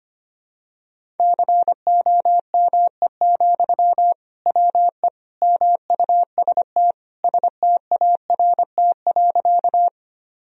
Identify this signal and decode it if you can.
Morse code, heard as a keyed tone, about 25 wpm, 700 Hz. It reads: COME, WE MUST START.
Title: Sentences from Top 1000 Words - Review Part 1 - 25wpm